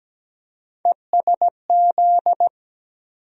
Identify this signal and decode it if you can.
Morse code, heard as a keyed tone, about 17 wpm, 700 Hz. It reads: ESZ
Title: USA Call Sign Postfixes - 3 Letters 17wpm